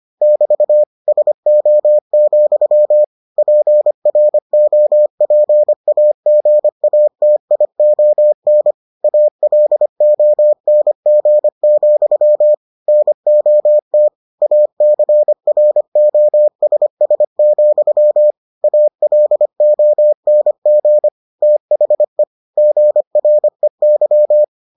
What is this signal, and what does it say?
Morse code, heard as a keyed tone, about 25 wpm, 600 Hz. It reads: <BT> SO, PROPAGATION ALONG, NOT ACROSS, ALONG THE GREY